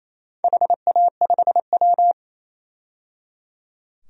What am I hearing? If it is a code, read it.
Morse code, heard as a keyed tone, about 28 wpm, 700 Hz. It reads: HA5W
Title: DXCC-100 Call Signs - 4 Characters - 28wpm